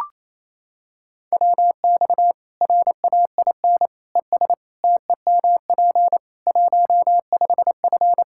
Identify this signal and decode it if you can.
Morse code, heard as a keyed tone, about 27 wpm, 700 Hz. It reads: WX RAIN ES TEMP 15F
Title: Practice QSO - Exchange 2 - Info - 28wpm